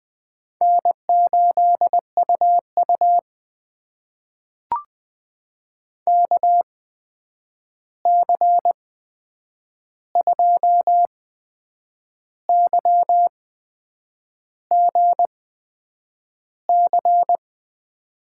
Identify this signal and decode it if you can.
Morse code, heard as a keyed tone, about 20 wpm, spaced as for 8 wpm, 700 Hz. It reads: N8UU KC2YGC
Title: USA Call Signs - 8x Character Spacing - 20wpm